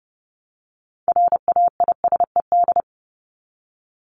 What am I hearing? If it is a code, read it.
Morse code, heard as a keyed tone, about 30 wpm, 700 Hz. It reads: RAISED
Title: ED Suffix - Top 100 Words with Base Words 30wpm